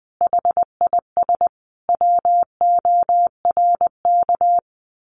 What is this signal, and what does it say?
Morse code, heard as a keyed tone, about 20 wpm, 700 Hz. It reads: HIS WORK
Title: Sentences from Top 1500 Words 20wpm